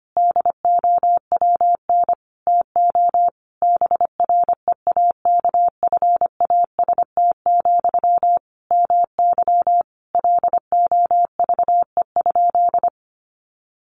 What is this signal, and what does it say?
Morse code, heard as a keyed tone, about 25 wpm, 700 Hz. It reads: DOWN TO BREAKFAST, MY LOVE?
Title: Sentences from Top 1500 Words 25wpm